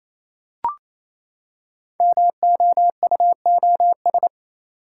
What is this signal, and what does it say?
Morse code, heard as a keyed tone, about 28 wpm, 700 Hz. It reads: MOUOS
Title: Code Groups - 5 Letters - 28wpm